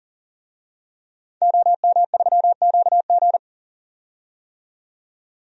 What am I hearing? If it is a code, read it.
Morse code, heard as a keyed tone, about 40 wpm, 700 Hz. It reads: OM3QG
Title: DXCC-100 Call Signs - 5 Characters - 40wpm